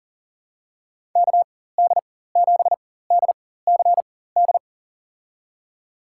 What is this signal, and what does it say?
Morse code, heard as a keyed tone, about 40 wpm, spaced as for 19 wpm, 700 Hz. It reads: KD7DCD
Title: USA Call Signs - 6 Characters - 4x Character Spacing - 40wpm